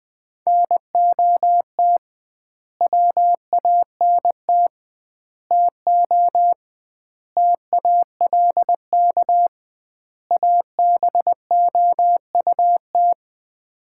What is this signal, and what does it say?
Morse code, heard as a keyed tone, about 20 wpm, 700 Hz. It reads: NOT WANT TO TALK ABOUT
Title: 2x Word-Spacing - Sets of 5 Words - 20wpm